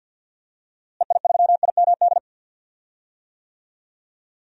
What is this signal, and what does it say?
Morse code, heard as a keyed tone, about 53 wpm, 700 Hz. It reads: EI3IMD
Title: DXCC-100 Call Signs - 6 Characters - Repeated 2x - 50wpm